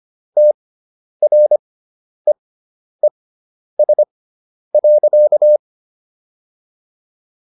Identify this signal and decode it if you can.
Morse code, heard as a keyed tone, about 25 wpm, spaced as for 10 wpm, 600 Hz. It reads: TREES.